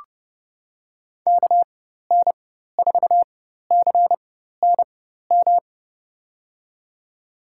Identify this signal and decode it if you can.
Morse code, heard as a keyed tone, about 30 wpm, spaced as for 14 wpm, 700 Hz. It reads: KN4CNM
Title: USA Call Signs - 6 Characters - 4x Character Spacing - 30wpm